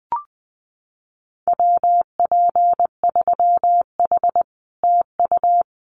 Morse code, heard as a keyed tone, about 20 wpm, 700 Hz. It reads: WP3H TU
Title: Common QSO Phrases Encore 20wpm